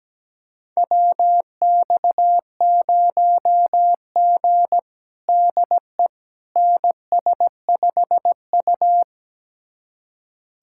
Morse code, heard as a keyed tone, about 17 wpm, 700 Hz. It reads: WX0G DE NS5U
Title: USA Callsign DE Callsign - 4 Characters - 17wpm